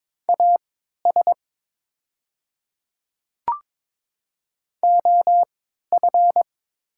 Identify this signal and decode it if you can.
Morse code, heard as a keyed tone, about 22 wpm, spaced as for 12 wpm, 700 Hz. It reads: AS OF
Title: 3x Character Spacing - Top 100 Words - 22wpm